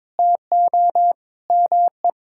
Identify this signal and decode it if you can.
Morse code, heard as a keyed tone, about 22 wpm, 700 Hz. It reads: TO ME
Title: Sentences from Top 1000 Words - Review Part 6 - 22wpm